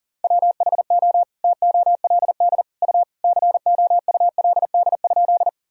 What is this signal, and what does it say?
Morse code, heard as a keyed tone, about 37 wpm, 700 Hz. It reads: WHO TOLD U COULD?
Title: Sentences from Top 400 Words and Abbreviations 40wpm